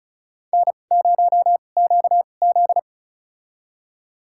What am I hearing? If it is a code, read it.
Morse code, heard as a keyed tone, about 36 wpm, spaced as for 25 wpm, 700 Hz. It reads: N0QZ
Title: USA Call Signs - 4 Characters - 2x Character Spacing - 35wpm